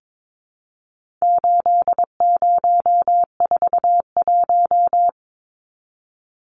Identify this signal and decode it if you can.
Morse code, heard as a keyed tone, about 22 wpm, 700 Hz. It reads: 8041
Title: Numbers - 4-Digit 22wpm